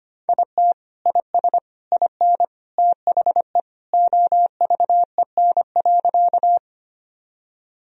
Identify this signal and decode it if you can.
Morse code, heard as a keyed tone, about 25 wpm, 690 Hz. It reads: IT IS IN THE OVEN.
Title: Sentences from Top 1000 Words - Review Part 4 - 25wpm